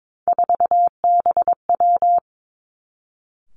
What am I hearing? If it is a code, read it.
Morse code, heard as a keyed tone, about 22 wpm, 700 Hz. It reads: 4BW
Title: Three Challenging Letter-Number Combinations 22wpm